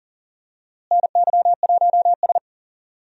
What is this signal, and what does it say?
Morse code, heard as a keyed tone, about 40 wpm, 700 Hz. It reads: NY1S
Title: USA Call Signs - 4 Characters - Repeated 2x - 40wpm